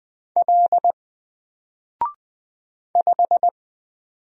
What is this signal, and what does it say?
Morse code, heard as a keyed tone, about 20 wpm, 700 Hz. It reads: L 5
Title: Single Letter-Number Alt 20wpm